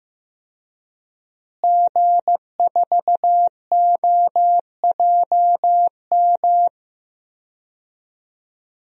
Morse code, heard as a keyed tone, about 15 wpm, 700 Hz. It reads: G4OJM
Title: DXCC-100 Call Signs - 5 Characters - Repeated 3x - 15wpm